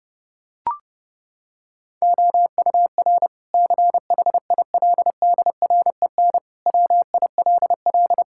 Morse code, heard as a keyed tone, about 32 wpm, 700 Hz. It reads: OUR CHILDREN WILL
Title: Sentences from Top 1000 Words - Review Part 5 - 30wpm